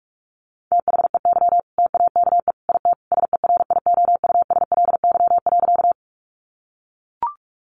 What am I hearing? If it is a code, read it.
Morse code, heard as a keyed tone, about 44 wpm, 700 Hz. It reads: THEY TAKE IT SERIOUSLY.
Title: Sentences from Top 1200 Words 45wpm